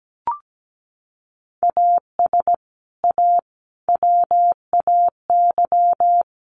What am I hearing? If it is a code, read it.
Morse code, heard as a keyed tone, about 17 wpm, 700 Hz. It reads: AS A WAY
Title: Sets of 4 Words 17wpm